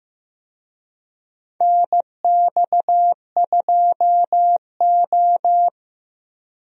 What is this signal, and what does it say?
Morse code, heard as a keyed tone, about 15 wpm, 700 Hz. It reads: NX2O
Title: USA Call Signs - 4 Characters - 15wpm